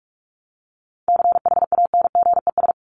Morse code, heard as a keyed tone, about 44 wpm, 700 Hz. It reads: CHANGES
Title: ES:S Suffix - Top 100 Words with Base Words 45wpm